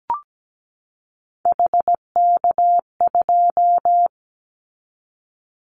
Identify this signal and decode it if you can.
Morse code, heard as a keyed tone, about 17 wpm, 700 Hz. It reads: HK2
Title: Callsign Prefixes for Top 100 DXCC - 17wpm